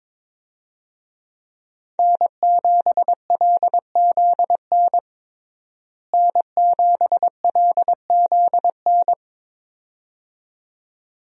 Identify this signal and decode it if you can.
Morse code, heard as a keyed tone, about 22 wpm, 700 Hz. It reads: N7LZN N7LZN
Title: USA Call Signs - 5 Characters - Repeated 2x - 22wpm